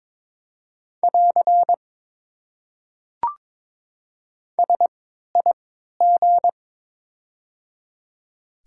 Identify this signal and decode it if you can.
Morse code, heard as a keyed tone, about 22 wpm, spaced as for 13 wpm, 700 Hz. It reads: <AR> SIG
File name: Top 90 QSO Elements - 3x Character Spacing - 22wpm